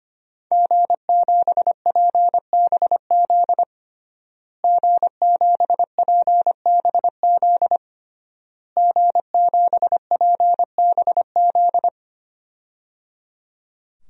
Morse code, heard as a keyed tone, about 25 wpm, 700 Hz. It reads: G7PBZ G7PBZ G7PBZ
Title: DXCC-100 Call Signs - 5 Characters - Repeated 3x - 25wpm